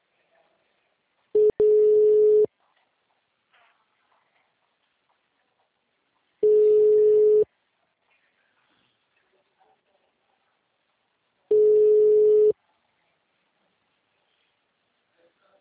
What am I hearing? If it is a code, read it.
Ringback tone (Europe, 425 Hz).